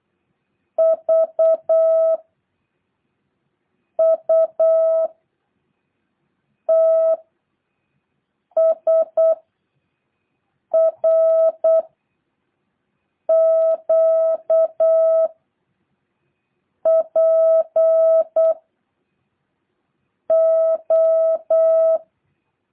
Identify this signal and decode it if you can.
Morse code, heard as a keyed tone, about 8 wpm, 650 Hz. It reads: V U T S R Q P O
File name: jh8cbh